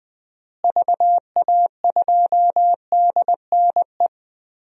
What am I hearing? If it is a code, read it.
Morse code, heard as a keyed tone, about 20 wpm, 700 Hz. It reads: VA2DNE
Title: DXCC-100 Call Signs - 6 Characters - Repeated 2x - 20wpm